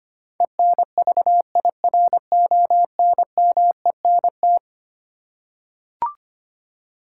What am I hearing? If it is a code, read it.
Morse code, heard as a keyed tone, about 25 wpm, 700 Hz. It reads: ENVIRONMENT
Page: morse-code-ninja-practice-25wpm